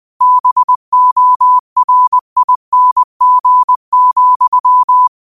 Morse code, heard as a keyed tone, about 20 wpm, 1000 Hz. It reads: BORING,
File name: qotdfiles / 20wpm